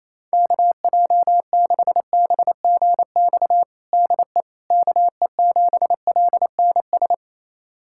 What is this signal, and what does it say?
Morse code, heard as a keyed tone, about 28 wpm, 700 Hz. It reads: KJ6BGX DE KE7LNS
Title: USA Callsign DE Callsign - 6 Characters - 28wpm